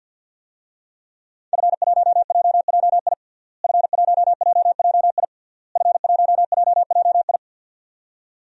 Morse code, heard as a keyed tone, about 51 wpm, 700 Hz. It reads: F1JJI F1JJI F1JJI
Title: DXCC-100 Call Signs - 5 Characters - Repeated 3x - 50wpm